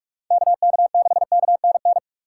Morse code, heard as a keyed tone, about 43 wpm, 700 Hz. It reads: KK6KNN